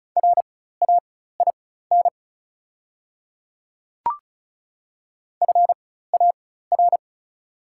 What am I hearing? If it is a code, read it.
Morse code, heard as a keyed tone, about 36 wpm, spaced as for 16 wpm, 700 Hz. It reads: RAIN FAR
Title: 4x Character Spacing - Top 300 Words - 35wpm